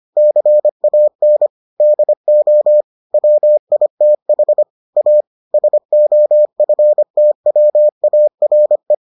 Morse code, heard as a keyed tone, about 25 wpm, 600 Hz. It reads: CAN DO WITH A SOFTWARE